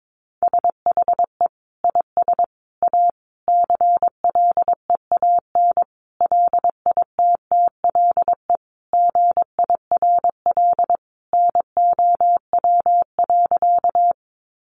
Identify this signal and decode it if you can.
Morse code, heard as a keyed tone, about 22 wpm, 700 Hz. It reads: SHE IS A CLEAN LITTLE GIRL NOW.
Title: Sentences from Top 1100 Words 22wpm